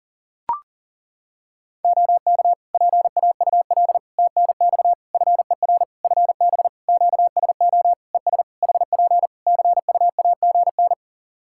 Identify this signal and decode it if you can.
Morse code, heard as a keyed tone, about 37 wpm, 700 Hz. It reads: OK PAUL TNX FER FB QSO ES HP CUAGN
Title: K1USN SST QSO Practice 40wpm